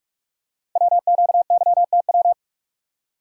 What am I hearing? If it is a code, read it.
Morse code, heard as a keyed tone, about 44 wpm, 690 Hz. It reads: WQYTW
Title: Code Groups - 5 Letters - 45wpm